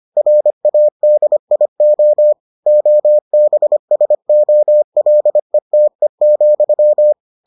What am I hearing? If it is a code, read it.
Morse code, heard as a keyed tone, about 25 wpm, 600 Hz. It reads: RADIO OBSOLETE,